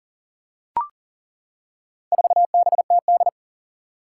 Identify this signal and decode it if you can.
Morse code, heard as a keyed tone, about 40 wpm, 700 Hz. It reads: 4BTD
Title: Four Challenging Letter-Number Combinations 40wpm